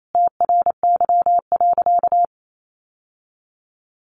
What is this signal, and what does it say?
Morse code, heard as a keyed tone, about 28 wpm, 700 Hz. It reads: TRY.
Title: Sentences from Top 600 Words 28wpm